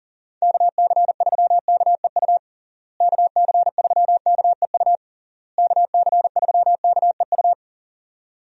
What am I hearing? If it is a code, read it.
Morse code, heard as a keyed tone, about 40 wpm, 700 Hz. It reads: KC3KEU KC3KEU KC3KEU
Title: USA Call Signs - 6 Characters - Repeated 3x - 40wpm